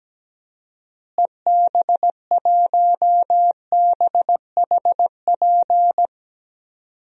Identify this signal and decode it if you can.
Morse code, heard as a keyed tone, about 17 wpm, 700 Hz. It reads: EB1BHP
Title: DXCC-100 Call Signs - 6 Characters - Repeated 2x - 17wpm